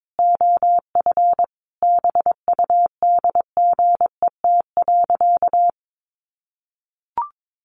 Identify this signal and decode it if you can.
Morse code, heard as a keyed tone, about 22 wpm, 700 Hz. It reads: OF BUDGET.
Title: Sentences from Top 1200 Words 22wpm